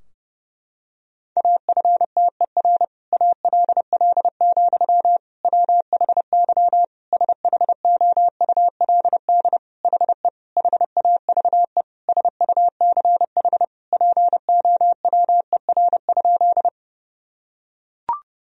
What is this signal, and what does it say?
Morse code, heard as a keyed tone, about 30 wpm, 700 Hz. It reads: AFTER ALL, WHY SHOULD HE HAVE SUCH POWER?
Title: Sentences from Top 1000 Words - Review Part 2 - 30wpm